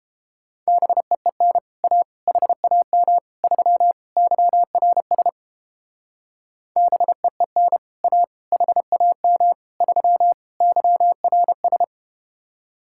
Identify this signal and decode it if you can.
Morse code, heard as a keyed tone, about 29 wpm, 700 Hz. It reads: BEEN A HAM 3 YRS BEEN A HAM 3 YRS